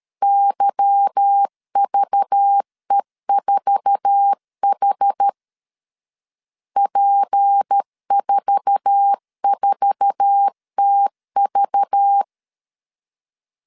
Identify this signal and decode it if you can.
Morse code, heard as a keyed tone, about 13 wpm, 800 Hz. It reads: YVE4H P44TV
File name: LEZIONI / 8 YJ45 / 8 YJ4 12wpm 2151